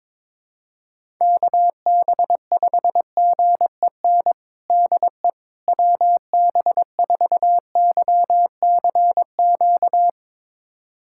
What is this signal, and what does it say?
Morse code, heard as a keyed tone, about 22 wpm, 700 Hz. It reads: KB5GEN DE WB4YCQ
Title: USA Callsign DE Callsign - 6 Characters - 22wpm